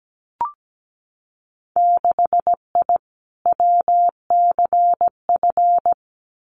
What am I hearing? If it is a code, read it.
Morse code, heard as a keyed tone, about 17 wpm, 700 Hz. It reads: 6I WCF
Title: Winter Field Day Exchanges Encore 17wpm